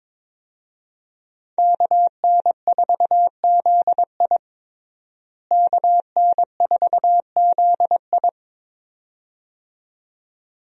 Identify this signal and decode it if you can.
Morse code, heard as a keyed tone, about 22 wpm, 700 Hz. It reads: KN4ZI KN4ZI
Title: USA Call Signs - 5 Characters - Repeated 2x - 22wpm